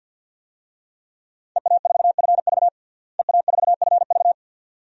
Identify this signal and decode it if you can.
Morse code, heard as a keyed tone, about 49 wpm, 700 Hz. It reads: EA4FV EA4FV
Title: DXCC-100 Call Signs - 5 Characters - Repeated 2x - 50wpm